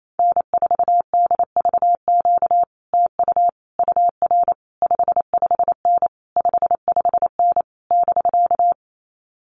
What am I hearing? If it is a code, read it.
Morse code, heard as a keyed tone, about 28 wpm, 700 Hz. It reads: N4DVQ TU UR 55N 55N <BK>
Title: Common QSO Phrases Encore 28wpm